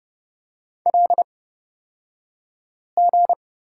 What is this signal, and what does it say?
Morse code, heard as a keyed tone, about 31 wpm, spaced as for 5 wpm, 700 Hz. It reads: LG
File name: Lesson 37d - Letter x - Two Characters 30wpm